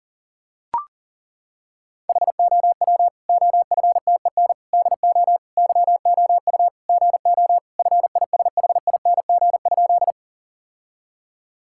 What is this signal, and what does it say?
Morse code, heard as a keyed tone, about 37 wpm, 700 Hz. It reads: HOW OFTEN DO YOU GO FISHING?
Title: Sentences from Top 1000 Words - Review Part 5 - 40wpm